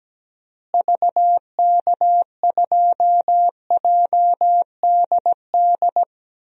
Morse code, heard as a keyed tone, about 17 wpm, 700 Hz. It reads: VK2JDD
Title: DXCC-100 Call Signs - 6 Characters - Repeated 2x - 17wpm